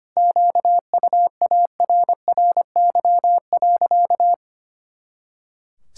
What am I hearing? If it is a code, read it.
Morse code, heard as a keyed tone, about 25 wpm, 700 Hz. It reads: QUARRY.
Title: Pangram Challenge v2 - 25wpm